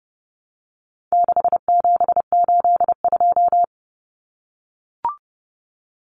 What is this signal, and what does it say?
Morse code, heard as a keyed tone, about 30 wpm, 700 Hz. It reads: 6782
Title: Common Cut Numbers - 4-Digit 30wpm